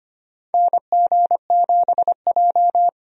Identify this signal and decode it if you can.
Morse code, heard as a keyed tone, about 25 wpm, 700 Hz. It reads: NG7J
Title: USA Call Signs - 4 Characters - 25wpm